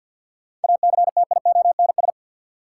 Morse code, heard as a keyed tone, about 52 wpm, 700 Hz. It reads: ACTIONS
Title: Sentences from Top 1000 Words - Review Part 6 - New Word Drill - 50wpm